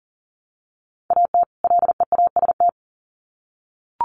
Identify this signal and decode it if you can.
Morse code, heard as a keyed tone, about 40 wpm, 700 Hz. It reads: AT LEAST
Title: Sets of 2 Words 40wpm